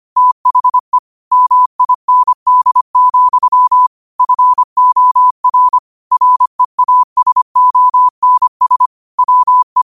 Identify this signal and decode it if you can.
Morse code, heard as a keyed tone, about 25 wpm, 1000 Hz. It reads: THE MIND, FOR REASONS WE